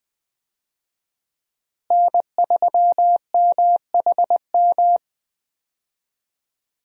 Morse code, heard as a keyed tone, about 20 wpm, 700 Hz. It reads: N3MHM
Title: USA Call Signs - 5 Characters - Repeated 3x - 20wpm